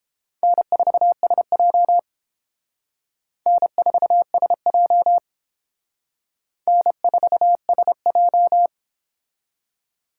Morse code, heard as a keyed tone, about 30 wpm, 700 Hz. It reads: N4SJ N4SJ N4SJ